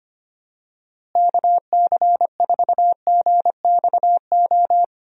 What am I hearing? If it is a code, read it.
Morse code, heard as a keyed tone, about 25 wpm, 700 Hz. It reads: KC4GXO
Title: USA Call Signs - 6 Characters - Repeated 3x - 25wpm